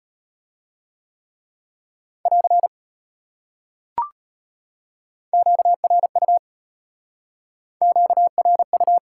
Morse code, heard as a keyed tone, about 31 wpm, 700 Hz. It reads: <AR> QRU QRU
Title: Speed-Racing Top 90 QSO Elements 25wpm